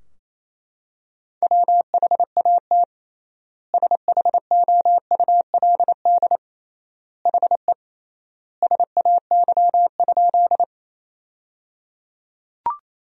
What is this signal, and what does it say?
Morse code, heard as a keyed tone, about 28 wpm, 700 Hz. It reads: WHAT SHOULD HE SAY?
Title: 3x Word-Spacing - Sentences from Top 300 Words - 28wpm